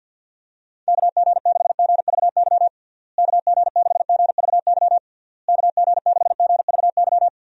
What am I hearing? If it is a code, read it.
Morse code, heard as a keyed tone, about 51 wpm, 700 Hz. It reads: KG6GVY KG6GVY KG6GVY
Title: USA Call Signs - 6 Characters - Repeated 3x - 50wpm